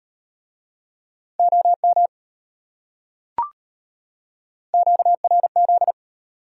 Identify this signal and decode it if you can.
Morse code, heard as a keyed tone, about 32 wpm, 700 Hz. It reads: OM QRZ